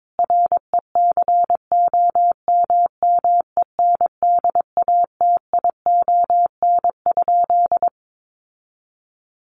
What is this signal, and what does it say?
Morse code, heard as a keyed tone, about 22 wpm, 700 Hz. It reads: RECOMMENDATION?
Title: Sentences from Top 1400 Words 22wpm